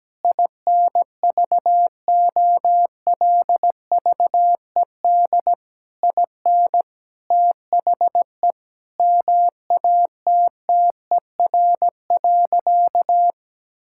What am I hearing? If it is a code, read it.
Morse code, heard as a keyed tone, about 17 wpm, 700 Hz. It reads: INVOLVED IN THE MATTER.
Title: Sentences from Top 1000 Words - Review Part 5 - 17wpm